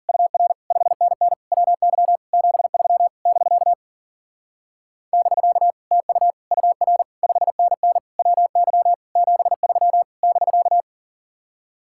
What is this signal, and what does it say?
Morse code, heard as a keyed tone, about 50 wpm, 700 Hz. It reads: UR 5NN WY 73 <BK> <BK> TU UR 5NN WY 73 <BK>
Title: Speed-Racing Common QSO Phrases - 40wpm